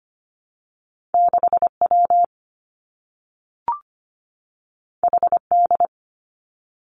Morse code, heard as a keyed tone, about 25 wpm, 700 Hz. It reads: 6 W H D
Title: Two Challenging Letter-Number Combinations 25wpm